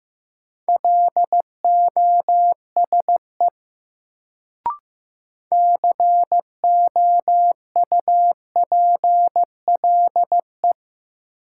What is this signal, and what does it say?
Morse code, heard as a keyed tone, about 15 wpm, 700 Hz. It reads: LOSE COUPLE
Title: Sentences from Top 1100 Words - New Word Drill - 15wpm